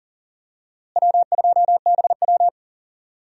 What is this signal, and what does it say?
Morse code, heard as a keyed tone, about 40 wpm, 700 Hz. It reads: W2BW
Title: USA Call Signs - 4 Characters - Repeated 3x - 40wpm